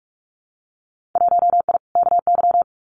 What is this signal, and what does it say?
Morse code, heard as a keyed tone, about 46 wpm, 700 Hz. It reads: 1I KY
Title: Winter Field Day Exchanges 45wpm